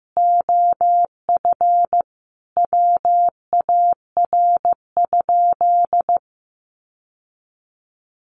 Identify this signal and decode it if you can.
Morse code, heard as a keyed tone, about 15 wpm, 700 Hz. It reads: OF WAR?
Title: Sentences from Top 1300 Words 15wpm